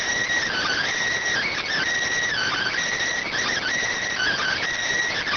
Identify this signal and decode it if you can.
Emergency Alert System (SAME) tones, heard as harsh buzzing bursts. It is heard from 0.5 to 4.6 s.